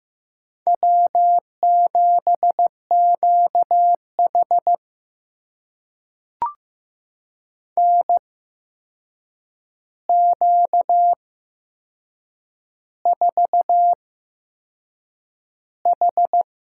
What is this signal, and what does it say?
Morse code, heard as a keyed tone, about 15 wpm, spaced as for 6 wpm, 700 Hz. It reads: W7QH NQ4H
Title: USA Call Signs - 8x Character Spacing - 15wpm